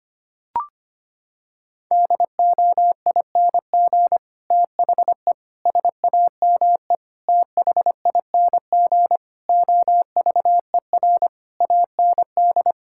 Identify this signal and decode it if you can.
Morse code, heard as a keyed tone, about 25 wpm, 700 Hz. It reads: DOING THE SAME THING OVER AND
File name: practice / 25wpm / Sentences with 9-Word Phrases - 25wpm